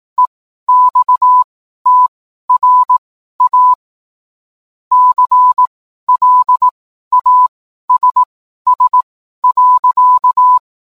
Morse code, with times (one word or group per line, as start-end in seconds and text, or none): 0.2-3.7 EXTRA
4.9-10.6 CLASS.